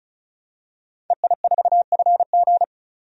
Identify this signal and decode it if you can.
Morse code, heard as a keyed tone, about 36 wpm, 700 Hz. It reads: EI4FG
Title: DXCC-100 Call Signs - 5 Characters - 35wpm